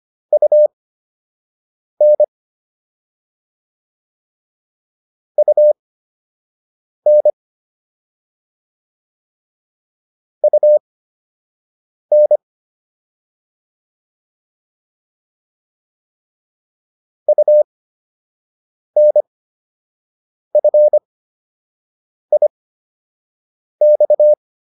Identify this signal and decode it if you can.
Morse code, heard as a keyed tone, about 25 wpm, spaced as for 6 wpm, 600 Hz. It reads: UN UN UN UNFIX
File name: UN5 06